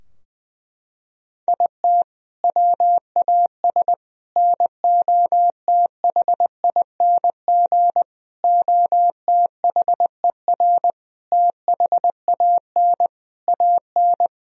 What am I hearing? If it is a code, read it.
Morse code, heard as a keyed tone, about 20 wpm, 700 Hz. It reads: IT WAS NOTHING OTHER THAN AN
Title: Sentences from Top 1000 Words - Review Part 2 - 20wpm